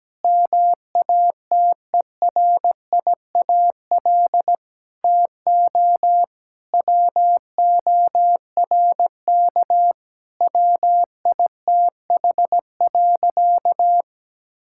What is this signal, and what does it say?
Morse code, heard as a keyed tone, about 17 wpm, 700 Hz. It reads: MATERIAL TO WORK WITH.
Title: Sentences from Top 1000 Words - Review Part 5 - 17wpm